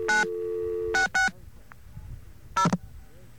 DTMF, heard as telephone touch tones, tones of about 140 ms. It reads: 9660
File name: mp3files / off the hook -128